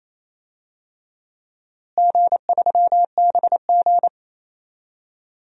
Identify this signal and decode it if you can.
Morse code, heard as a keyed tone, about 28 wpm, 700 Hz. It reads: G3BG
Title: DXCC-100 Call Signs - 4 Characters - Repeated 2x - 28wpm